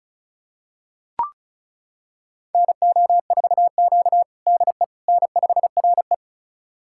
Morse code, heard as a keyed tone, about 36 wpm, 700 Hz. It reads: NO4Q DE N5RE
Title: USA Callsign DE Callsign - 4 Characters - 35wpm